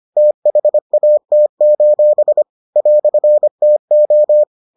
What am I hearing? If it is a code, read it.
Morse code, heard as a keyed tone, about 25 wpm, 600 Hz. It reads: THAT: "TO